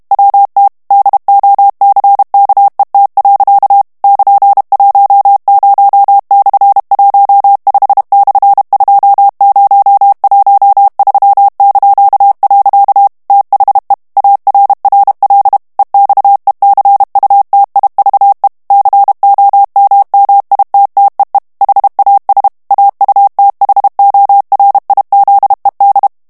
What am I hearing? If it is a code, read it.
Morse code, heard as a keyed tone, about 32 wpm, 800 Hz. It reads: WT DOCKET. <KN>10/15/2013). THE ARRL EXECUTIVE COMMITTEE HAS AUTHORIZED
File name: morse mp3 / ANGLAIS texte clair 3 vitesse 30 mots minute